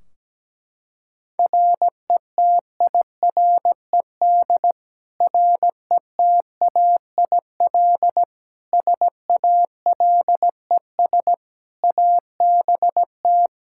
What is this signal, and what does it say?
Morse code, heard as a keyed tone, about 17 wpm, 700 Hz. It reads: RETIRED RETAIL SALES ABT
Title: Practice QSO - Exchange 3 - More Info - 17wpm